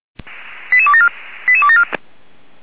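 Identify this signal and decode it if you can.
Five-tone selective call (ZVEI), heard as quick runs of five tones, tones of about 70 ms. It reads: 88274 88274
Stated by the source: ZVEI five-tone selective call received by SDR.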